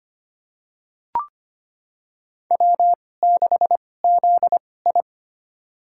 Morse code, heard as a keyed tone, about 25 wpm, spaced as for 18 wpm, 700 Hz. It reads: W6ZI
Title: USA Call Signs - 4 Characters - 2x Character Spacing - 25wpm